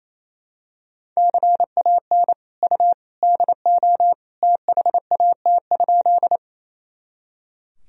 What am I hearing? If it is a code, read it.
Morse code, heard as a keyed tone, about 28 wpm, 700 Hz. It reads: CAN U DO THAT?
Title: Sentences from Top 100 Words and Abbreviations 28wpm